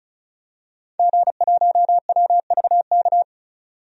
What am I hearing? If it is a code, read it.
Morse code, heard as a keyed tone, about 36 wpm, 700 Hz. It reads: G1WVK
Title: DXCC-100 Call Signs - 5 Characters - 35wpm